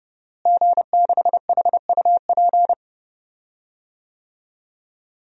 Morse code, heard as a keyed tone, about 31 wpm, 700 Hz. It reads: G6HUP
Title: DXCC-100 Call Signs - 5 Characters - Repeated 2x - 30wpm